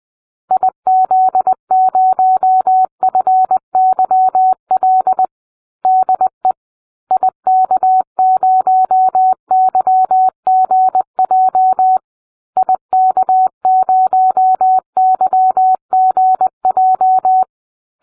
Morse code, heard as a keyed tone, about 20 wpm, 750 Hz. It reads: IZ0FYL DE IK0YGJ IK0YGJ